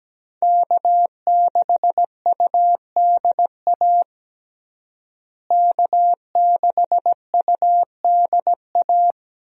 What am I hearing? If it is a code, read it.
Morse code, heard as a keyed tone, about 17 wpm, 700 Hz. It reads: K6UDA K6UDA
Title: USA Call Signs - 5 Characters - Repeated 2x - 17wpm